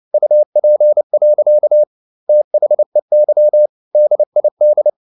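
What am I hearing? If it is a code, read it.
Morse code, heard as a keyed tone, about 29 wpm, 600 Hz. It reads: UP. THEY DID
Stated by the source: Morse code at 29 wpm, sending Ken Follett's novel Never in English, from CW med Gnister.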